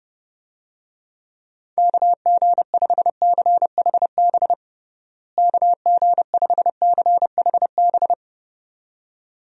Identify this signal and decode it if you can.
Morse code, heard as a keyed tone, about 30 wpm, 700 Hz. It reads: KG5CHB KG5CHB
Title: USA Call Signs - 6 Characters - Repeated 2x - 30wpm